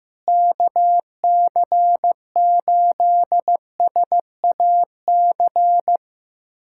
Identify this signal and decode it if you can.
Morse code, heard as a keyed tone, about 15 wpm, 700 Hz. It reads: KC8SAC
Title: USA Callsign DE Callsign 15wpm